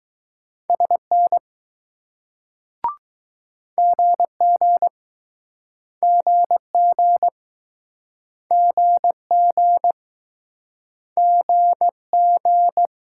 Morse code, heard as a keyed tone, about 20 wpm, 700 Hz. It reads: SN GG GG GG GG